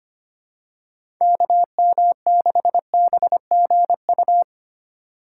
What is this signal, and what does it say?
Morse code, heard as a keyed tone, about 25 wpm, 700 Hz. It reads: KM6BGU
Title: USA Call Signs - 6 Characters - 4x Character Spacing - 25wpm